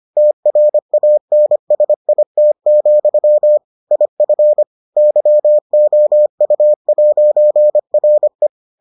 Morse code, heard as a keyed tone, about 25 wpm, 600 Hz. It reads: TRANSIT, IF YOU'RE